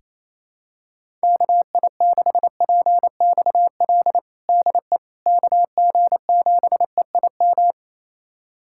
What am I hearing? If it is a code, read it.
Morse code, heard as a keyed tone, about 28 wpm, 700 Hz. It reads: KI6PXL DE KG7EIM
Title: USA Callsign DE Callsign 28wpm